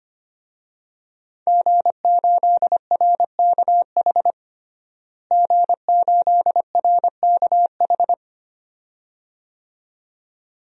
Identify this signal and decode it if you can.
Morse code, heard as a keyed tone, about 25 wpm, 700 Hz. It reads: G8RKH G8RKH
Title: DXCC-100 Call Signs - 5 Characters - Repeated 2x - 25wpm